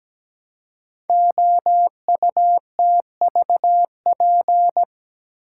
Morse code, heard as a keyed tone, about 17 wpm, 700 Hz. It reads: OUTVP
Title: Code Groups - 5 Letter:Numbers - 17wpm